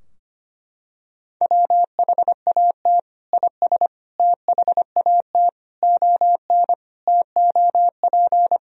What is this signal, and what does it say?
Morse code, heard as a keyed tone, about 25 wpm, 700 Hz. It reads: WHAT IS THAT ON TOP
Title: Sentences from Top 1000 Words - Review Part 2 - 25wpm